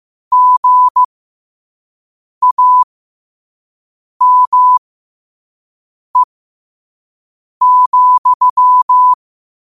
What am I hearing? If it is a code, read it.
Morse code, heard as a keyed tone, about 15 wpm, spaced as for 5 wpm, 1000 Hz. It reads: GAME,
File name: qotdfiles / 5wpm